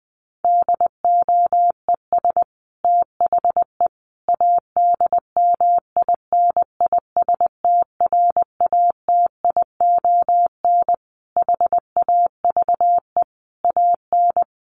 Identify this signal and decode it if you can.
Morse code, heard as a keyed tone, about 20 wpm, 700 Hz. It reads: DOES THE ADMINISTRATION HAVE AN
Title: Sentences from Top 1300 Words 20wpm